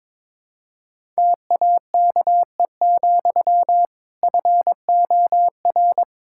Morse code, heard as a keyed tone, about 22 wpm, 700 Hz. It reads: TAKE, FOR
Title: Sentences from Top 1000 Words - Review Part 5 - 22wpm